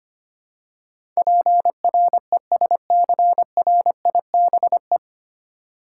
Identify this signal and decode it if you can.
Morse code, heard as a keyed tone, about 25 wpm, 700 Hz. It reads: PRESCRIBE
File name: Nine Letter Words - 25wpm